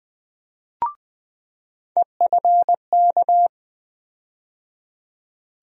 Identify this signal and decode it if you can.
Morse code, heard as a keyed tone, about 20 wpm, 700 Hz. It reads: EFK
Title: USA Call Sign Postfixes - 3 Letters 20wpm